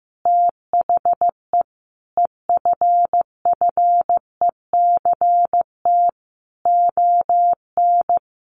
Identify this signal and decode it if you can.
Morse code, heard as a keyed tone, about 15 wpm, 700 Hz. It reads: THE EFFECT ON
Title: Sentences from Top 1300 Words 15wpm